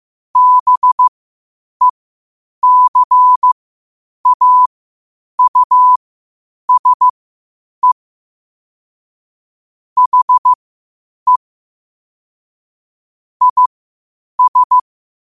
Morse code, heard as a keyed tone, about 15 wpm, spaced as for 8 wpm, 1000 Hz. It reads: BECAUSE HE IS
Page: qotd-at-8-wpm